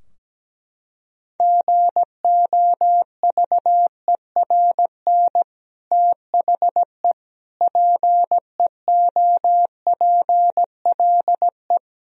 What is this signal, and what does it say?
Morse code, heard as a keyed tone, about 17 wpm, 700 Hz. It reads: GOVERN THE PEOPLE
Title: Sentences from Top 1000 Words - Review Part 2 - 17wpm